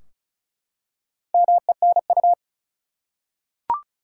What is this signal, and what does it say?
Morse code, heard as a keyed tone, about 36 wpm, 700 Hz. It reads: MENU
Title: Four Letter Words - 35wpm